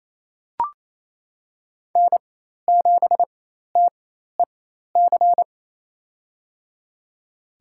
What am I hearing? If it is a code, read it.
Morse code, heard as a keyed tone, about 28 wpm, spaced as for 13 wpm, 700 Hz. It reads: N7TEC
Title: USA Call Signs - 5 Characters - 4x Character Spacing - 28wpm